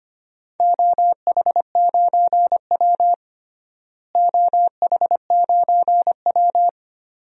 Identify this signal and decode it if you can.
Morse code, heard as a keyed tone, about 25 wpm, 700 Hz. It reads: OH9W OH9W
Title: DXCC-100 Call Signs - 4 Characters - Repeated 2x - 25wpm